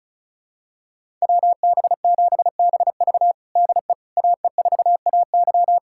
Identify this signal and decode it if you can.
Morse code, heard as a keyed tone, about 36 wpm, 700 Hz. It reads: WB7BV DE AE4AY
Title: USA Callsign DE Callsign - 5 Characters - 35wpm